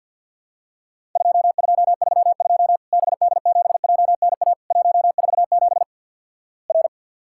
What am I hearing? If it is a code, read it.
Morse code, heard as a keyed tone, about 52 wpm, 700 Hz. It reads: 2232 BD7JNA 147 R
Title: Worked All Europe DX Contest - QTC Exchange with Repeats - 50wpm